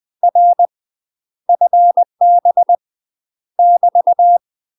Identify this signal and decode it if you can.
Morse code, heard as a keyed tone, about 20 wpm, 700 Hz. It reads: R FB <BT>
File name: UK MASTER QSOs TRAINING / 20WPM / UK MASTER QSOs 0055